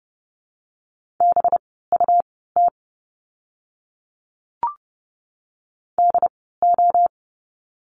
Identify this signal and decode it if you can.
Morse code, heard as a keyed tone, about 30 wpm, spaced as for 17 wpm, 700 Hz. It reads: BUT DO
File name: course / 30:10wpm / 237 Lesson 38f - Letter q - Words 30wpm